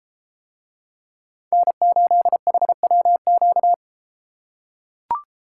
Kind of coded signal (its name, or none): Morse code